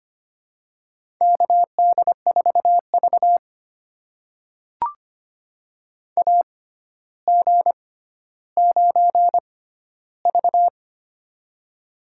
Morse code, heard as a keyed tone, about 25 wpm, spaced as for 13 wpm, 700 Hz. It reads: KD4V AG9V
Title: USA Call Signs - 4 Characters - 6x Character Spacing - 25wpm